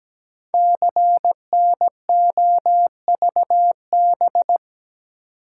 Morse code, heard as a keyed tone, about 17 wpm, 700 Hz. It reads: CNOVB
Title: Code Groups - 5 Letters - 17wpm